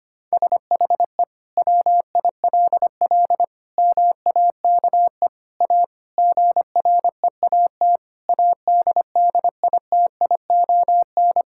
Morse code, heard as a keyed tone, about 25 wpm, 700 Hz. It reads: SHE WILL MAKE A GREAT ADDITION
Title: Sentences from Top 1000 Words - Review Part 6 - 25wpm